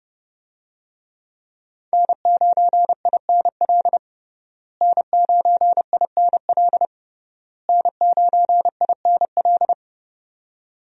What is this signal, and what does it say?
Morse code, heard as a keyed tone, about 30 wpm, 700 Hz. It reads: N9INL N9INL N9INL
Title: USA Call Signs - 5 Characters - Repeated 3x - 30wpm